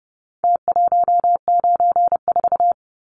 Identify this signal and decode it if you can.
Morse code, heard as a keyed tone, about 29 wpm, 700 Hz. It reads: T194
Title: Numbers - Contest Style - 4-Digit 30wpm